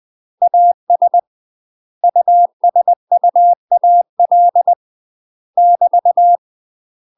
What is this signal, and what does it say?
Morse code, heard as a keyed tone, about 20 wpm, 700 Hz. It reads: AS USUAL <BT>
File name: UK MASTER QSOs TRAINING / 20WPM / UK MASTER QSOs 0049